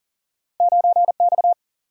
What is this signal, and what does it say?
Morse code, heard as a keyed tone, about 42 wpm, 700 Hz. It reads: 9X
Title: Single-Letter Single-Number 40wpm